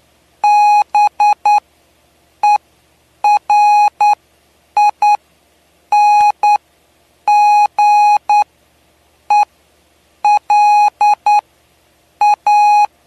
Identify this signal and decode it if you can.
Morse code, heard as a keyed tone, about 9 wpm, spaced as for 7 wpm, 830 Hz. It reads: BERINGELA
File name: cw / py2lcd / 07-PY2LCD-Li��o 7